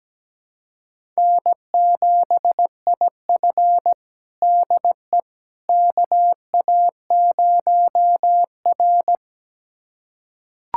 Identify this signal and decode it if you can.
Morse code, heard as a keyed tone, about 17 wpm, 700 Hz. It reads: N7IF DE KA0R
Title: USA Callsign DE Callsign - 4 Characters - 17wpm